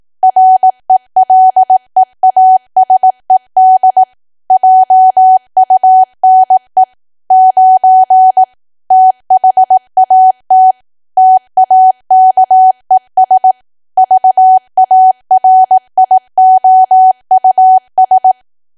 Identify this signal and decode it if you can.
Morse code, heard as a keyed tone, about 18 wpm, 750 Hz. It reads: RELEASED JUNE 9 THAT TAKES VARIOUS
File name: cwa / W1AW-Level2 / 18WPM